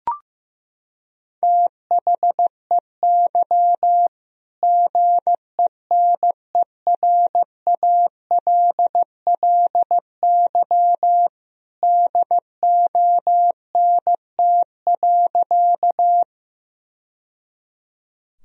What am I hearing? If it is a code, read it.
Morse code, heard as a keyed tone, about 15 wpm, 700 Hz. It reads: THEY GENERALLY DONT.
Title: Sentences from Top 1000 Words - Review Part 4 - 15wpm